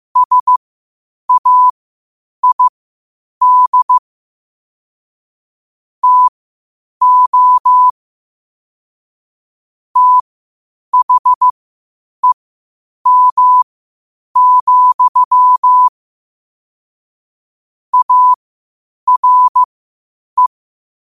Morse code, heard as a keyed tone, about 15 wpm, spaced as for 8 wpm, 1000 Hz. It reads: SAID TO THEM, ARE
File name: qotdfiles / 8wpm